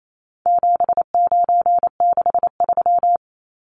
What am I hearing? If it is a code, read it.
Morse code, heard as a keyed tone, about 28 wpm, 700 Hz. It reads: 7963
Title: Numbers - 4-Digit 28wpm